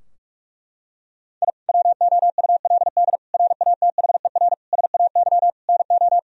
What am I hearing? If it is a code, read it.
Morse code, heard as a keyed tone, about 42 wpm, 700 Hz. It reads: I WOULD RATHER SAY NO
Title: Sentences from Top 1000 Words - Review Part 4 - 45wpm